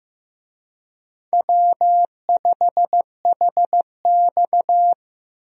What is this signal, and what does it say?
Morse code, heard as a keyed tone, about 15 wpm, 700 Hz. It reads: W5HX
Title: USA Call Signs - 4 Characters - 15wpm